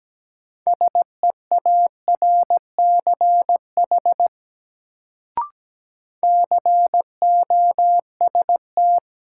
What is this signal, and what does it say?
Morse code, heard as a keyed tone, about 17 wpm, 700 Hz. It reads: SEARCH COST
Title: Sentences from Top 800 Words - New Word Drill - 17wpm